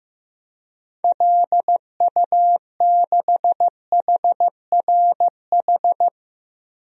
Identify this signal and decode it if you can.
Morse code, heard as a keyed tone, about 15 wpm, 700 Hz. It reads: LU6HRH
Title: DXCC-100 Call Signs - 6 Characters - 15wpm